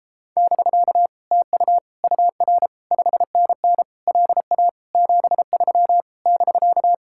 Morse code, heard as a keyed tone, about 33 wpm, 700 Hz. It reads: <BK> TU UR 5NN LA 73 <BK>